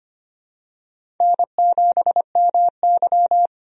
Morse code, heard as a keyed tone, about 25 wpm, 700 Hz. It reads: N7MY
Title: USA Call Signs - 4 Characters - 25wpm